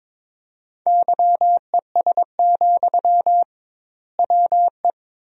Morse code, heard as a keyed tone, about 22 wpm, 700 Hz. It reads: YES, WE
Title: 2x Word-Spacing - Sentences from Top 500 Words - 22wpm